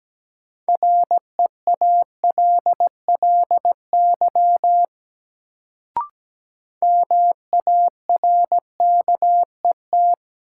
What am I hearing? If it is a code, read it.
Morse code, heard as a keyed tone, about 17 wpm, 700 Hz. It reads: REALLY MARKET
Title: Sentences from Top 600 Words - New Word Drill - 17wpm